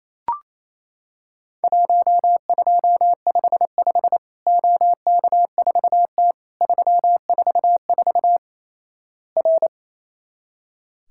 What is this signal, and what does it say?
Morse code, heard as a keyed tone, about 28 wpm, 700 Hz. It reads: 1255 OK4T 344 R
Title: Worked All Europe DX Contest - QTC Exchange - 28wpm